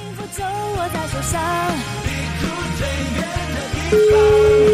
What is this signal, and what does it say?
Ringback tone (Europe, 425 Hz), with noise.